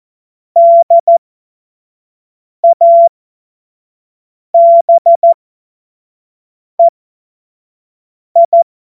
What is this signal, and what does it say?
Morse code, heard as a keyed tone, about 14 wpm, spaced as for 5 wpm, 680 Hz. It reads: DABEI
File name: Morsekurs / Witze / Witz 13 Februar 2024 25 F70